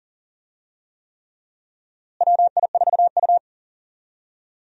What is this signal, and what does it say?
Morse code, heard as a keyed tone, about 40 wpm, 700 Hz. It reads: WI4U
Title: USA Call Signs - 4 Characters - Repeated 3x - 40wpm